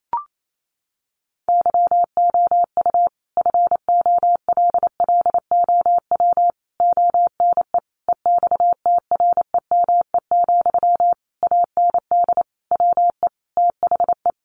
Morse code, heard as a keyed tone, about 28 wpm, 700 Hz. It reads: YOU FOLLOW ONE EXTREME, AND WE THE
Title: Sentences from Top 1300 Words 28wpm